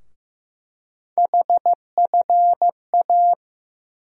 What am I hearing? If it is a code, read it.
Morse code, heard as a keyed tone, about 15 wpm, 700 Hz. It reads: HFA
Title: USA Call Sign Postfixes - 3 Letters 15wpm